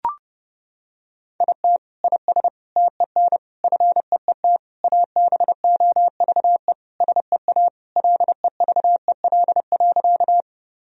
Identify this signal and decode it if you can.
Morse code, heard as a keyed tone, about 32 wpm, 700 Hz. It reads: IT IS TEN FEET ABOVE SEA LEVEL.
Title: Sentences from Top 1000 Words - Review Part 4 - 30wpm